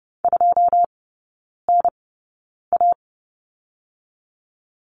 Morse code, heard as a keyed tone, about 30 wpm, spaced as for 9 wpm, 700 Hz. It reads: 2NA
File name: 081 Lesson 14e - Number 2 - Three Characters 30wpm